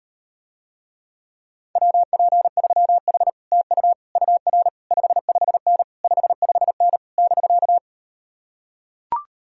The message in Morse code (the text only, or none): WP3H TU UR 55N 55N <BK>